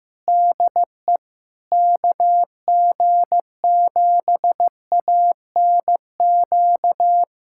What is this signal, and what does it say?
Morse code, heard as a keyed tone, about 15 wpm, 700 Hz. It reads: DE KG7ANQ
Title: USA Callsign DE Callsign - 6 Characters - 15wpm